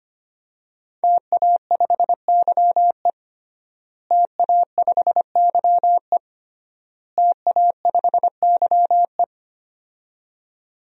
Morse code, heard as a keyed tone, about 25 wpm, 700 Hz. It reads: TA5YE TA5YE TA5YE
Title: DXCC-100 Call Signs - 5 Characters - Repeated 3x - 25wpm